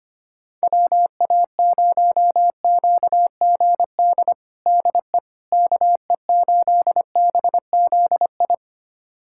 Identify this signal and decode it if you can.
Morse code, heard as a keyed tone, about 25 wpm, 700 Hz. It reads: WA0QGD DE KE8BZI
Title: USA Callsign DE Callsign 25wpm